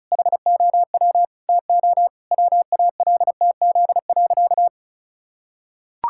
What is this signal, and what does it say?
Morse code, heard as a keyed tone, about 36 wpm, 700 Hz. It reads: HOW TO WALTZ.
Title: Pangram Challenge v2 - 35wpm